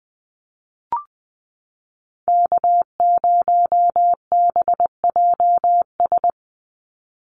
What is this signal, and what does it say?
Morse code, heard as a keyed tone, about 20 wpm, 700 Hz. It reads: K0BJS
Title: USA Call Signs 20wpm